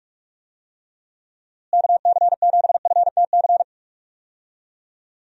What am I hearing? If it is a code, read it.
Morse code, heard as a keyed tone, about 46 wpm, 700 Hz. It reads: KC7FTC
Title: USA Call Signs - 6 Characters - Repeated 3x - 45wpm